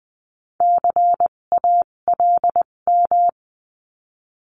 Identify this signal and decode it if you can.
Morse code, heard as a keyed tone, about 20 wpm, spaced as for 17 wpm, 700 Hz. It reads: CALM
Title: 257 Lesson 42b - Words 14-20wpm